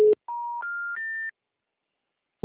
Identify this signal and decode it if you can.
Special information tone.